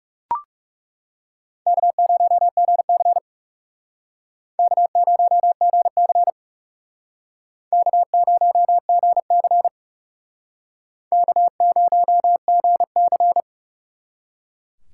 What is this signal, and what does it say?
Morse code, heard as a keyed tone, about 36 wpm, 700 Hz. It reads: K0GC K0GC K0GC K0GC
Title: Speed Racing - USA 1x2 Call Signs 30wpm